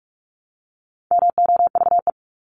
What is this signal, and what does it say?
Morse code, heard as a keyed tone, about 44 wpm, 700 Hz. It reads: MOVE
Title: ING Suffix - Top 100 Words with Base Words 45wpm